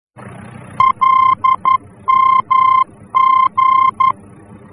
Morse code, heard as a keyed tone, about 11 wpm, 1050 Hz. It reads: LMG
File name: france / mx / LMG-F-VOR